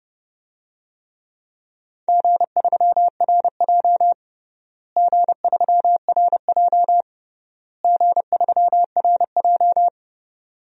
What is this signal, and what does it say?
Morse code, heard as a keyed tone, about 32 wpm, 700 Hz. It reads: G3RJ G3RJ G3RJ
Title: DXCC-100 Call Signs - 4 Characters - Repeated 3x - 30wpm